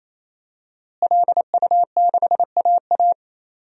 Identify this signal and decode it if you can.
Morse code, heard as a keyed tone, about 27 wpm, 700 Hz. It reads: LU6AA
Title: DXCC-100 Call Signs - 5 Characters - Repeated 3x - 28wpm